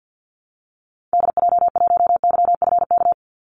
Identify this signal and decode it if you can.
Morse code, heard as a keyed tone, about 50 wpm, 700 Hz. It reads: DJ1YFK
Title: Call Signs - 50wpm